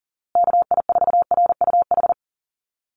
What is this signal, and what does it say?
Morse code, heard as a keyed tone, about 40 wpm, 700 Hz. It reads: KI4RUH
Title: USA Call Signs Repeated 2x 40wpm